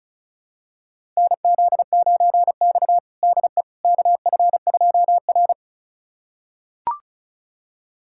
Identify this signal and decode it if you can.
Morse code, heard as a keyed tone, about 36 wpm, 700 Hz. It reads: NZ9X DE KF2R
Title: USA Callsign DE Callsign - 4 Characters - 35wpm